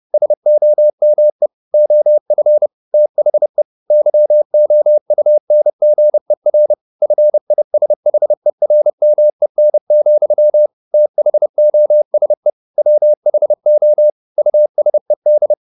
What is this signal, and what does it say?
Morse code, heard as a keyed tone, about 30 wpm, 600 Hz. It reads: SOME OF THE YOUNGER FISHERMEN, THOSE WHO USED
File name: the old man and the sea / the old man and the sea 1 30wpm